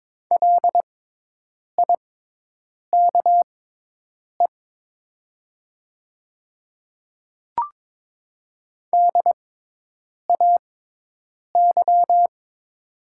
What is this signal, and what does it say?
Morse code, heard as a keyed tone, about 22 wpm, spaced as for 8 wpm, 700 Hz. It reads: LIKE DAY
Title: 6x Character Spacing - Top 100 Words - 22wpm